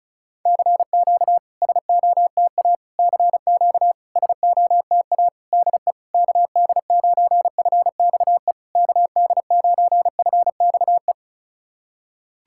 Morse code, heard as a keyed tone, about 34 wpm, 660 Hz. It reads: CQ SOTA CQ SOTA DE KD9FXE KD9FXE
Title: SOTA QSO Practice 35wpm